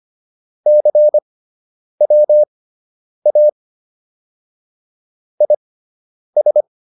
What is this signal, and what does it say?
Morse code, heard as a keyed tone, about 25 wpm, spaced as for 9 wpm, 600 Hz. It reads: CWA IS